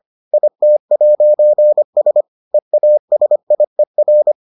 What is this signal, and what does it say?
Morse code, heard as a keyed tone, about 25 wpm, 600 Hz. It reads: IT'S EASIER